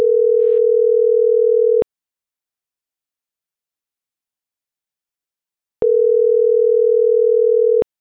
Ringback tone (North America), 440 + 480 Hz.